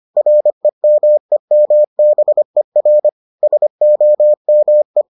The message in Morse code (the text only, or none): REMEMBER SOME